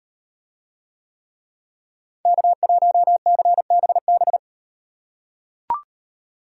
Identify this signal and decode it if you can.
Morse code, heard as a keyed tone, about 38 wpm, 700 Hz. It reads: K1CBB